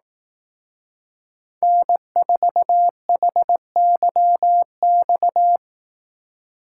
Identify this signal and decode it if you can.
Morse code, heard as a keyed tone, about 18 wpm, 700 Hz. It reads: N4HYX
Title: Speed Racing - USA 1x3 Call Signs 15wpm